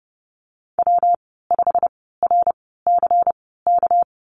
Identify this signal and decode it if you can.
Morse code, heard as a keyed tone, about 30 wpm, spaced as for 17 wpm, 700 Hz. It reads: W5RCK
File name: Lesson 37g - Letter x - Call Signs 30wpm